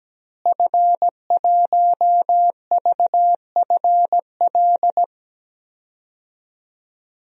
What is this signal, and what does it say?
Morse code, heard as a keyed tone, about 17 wpm, 700 Hz. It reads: F1VFL
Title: DXCC-100 Call Signs - 5 Characters - Repeated 3x - 17wpm